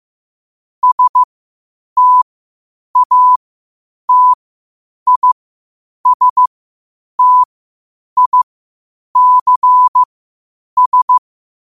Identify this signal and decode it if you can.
Morse code, heard as a keyed tone, about 15 wpm, spaced as for 8 wpm, 1000 Hz. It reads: STATISTICS
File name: qotdfiles / 8wpm